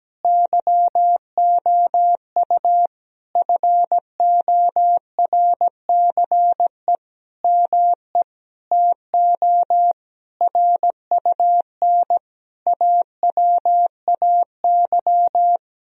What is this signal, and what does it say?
Morse code, heard as a keyed tone, about 17 wpm, 700 Hz. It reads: YOU FORCE ME TO RUN AWAY
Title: Sentences from Top 1000 Words - Review Part 3 - 17wpm